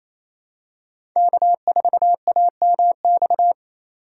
Morse code, heard as a keyed tone, about 28 wpm, 700 Hz. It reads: K4AMX
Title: USA Call Signs - 5 Characters - 2x Character Spacing - 28wpm